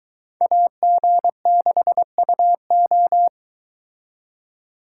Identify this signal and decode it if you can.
Morse code, heard as a keyed tone, about 22 wpm, 700 Hz. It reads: AG6UO